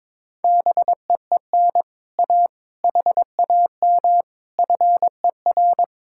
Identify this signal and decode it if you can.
Morse code, heard as a keyed tone, about 22 wpm, 700 Hz. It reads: BEEN A HAM FER
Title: Speed-Racing Common QSO Phrases Encore - 22wpm